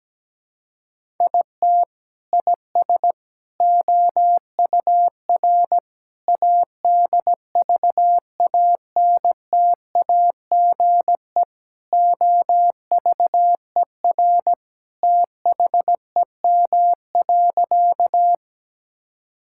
Morse code, heard as a keyed tone, about 17 wpm, 700 Hz. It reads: IT IS OUR ADVANTAGE OVER THEM.
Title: Sentences from Top 1000 Words - Review Part 6 - 17wpm